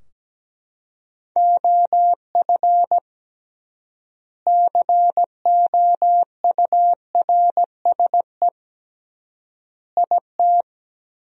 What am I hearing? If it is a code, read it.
Morse code, heard as a keyed tone, about 17 wpm, 700 Hz. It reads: OF COURSE IT
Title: 3x Word-Spacing - Sentences from Top 500 Words - 17wpm